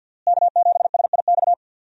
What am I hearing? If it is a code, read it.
Morse code, heard as a keyed tone, about 52 wpm, 700 Hz. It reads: K7SIX
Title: USA Call Signs - 5 Characters - Repeated 2x - 50wpm